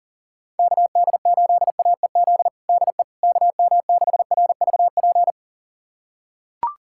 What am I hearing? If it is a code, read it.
Morse code, heard as a keyed tone, about 40 wpm, 700 Hz. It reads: KD8AEZ DE KM6RVP
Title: USA Callsign DE Callsign - 6 Characters - 40wpm